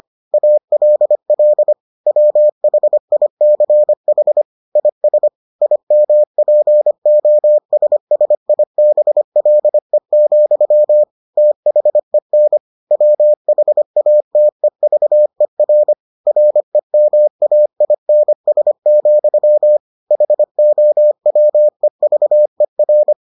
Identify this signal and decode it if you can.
Morse code, heard as a keyed tone, about 25 wpm, 600 Hz. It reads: ALL WHICH IS IMPOSSIBLE, THEN WHATEVER REMAINS, HOWEVER